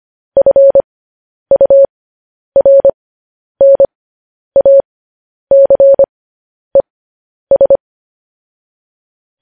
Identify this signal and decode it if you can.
Morse code, heard as a keyed tone, about 25 wpm, spaced as for 10 wpm, 550 Hz. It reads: FURNACES
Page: News Headlines in Morse Code at 10 WPM